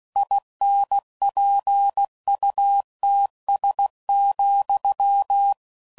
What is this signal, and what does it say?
Morse code, heard as a keyed tone, about 16 wpm, 800 Hz. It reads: INPUTS,
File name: CW Code Groups @15WPM / Technical Subjects / AND Gate